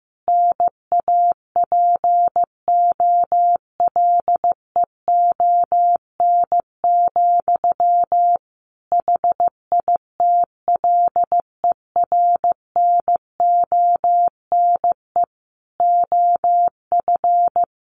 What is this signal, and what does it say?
Morse code, heard as a keyed tone, about 15 wpm, 700 Hz. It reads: NAPOLEON, HITLER NONE OF
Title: A Sound Of Thunder 15wpm